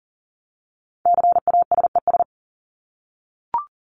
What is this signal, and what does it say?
Morse code, heard as a keyed tone, about 38 wpm, 700 Hz. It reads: CASES
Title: ES:S Suffix - Top 100 Words with Base Words 40wpm